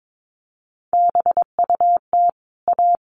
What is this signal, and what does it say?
Morse code, heard as a keyed tone, about 22 wpm, 700 Hz. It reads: BUT A
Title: Sentences from Top 1400 Words 22wpm